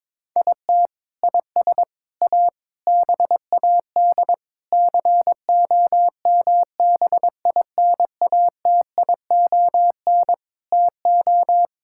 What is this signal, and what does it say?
Morse code, heard as a keyed tone, about 22 wpm, 700 Hz. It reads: IT IS A BAD COMBINATION TO
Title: Sentences from Top 1000 Words - Review Part 6 - 22wpm